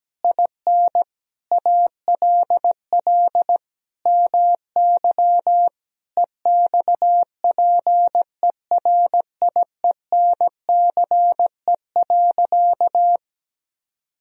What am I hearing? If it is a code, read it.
Morse code, heard as a keyed tone, about 17 wpm, 700 Hz. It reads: IN ALL MY EXPERIENCE.
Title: Sentences from Top 1000 Words - Review Part 3 - 17wpm